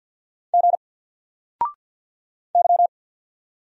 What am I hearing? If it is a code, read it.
Morse code, heard as a keyed tone, about 48 wpm, 700 Hz. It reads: G Y
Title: Single Character (Letter:Number:Punctuation:Prosign) Alt 50wpm